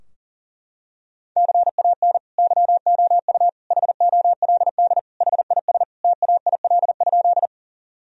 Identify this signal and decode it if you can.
Morse code, heard as a keyed tone, about 40 wpm, 700 Hz. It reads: CAN YOU HOLD HIS TAIL?
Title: Sentences from Top 1000 Words - Review Part 3 - 40wpm